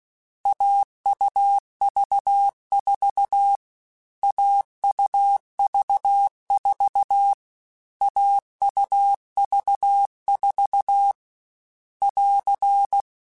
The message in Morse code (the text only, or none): AUV4 AUV4 AUV4 <AR>